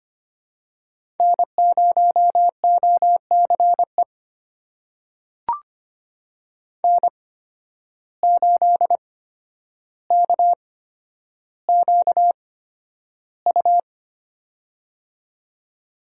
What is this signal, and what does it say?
Morse code, heard as a keyed tone, about 25 wpm, spaced as for 11 wpm, 700 Hz. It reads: N0OCE N8KQU
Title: USA Call Signs - 8x Character Spacing - 25wpm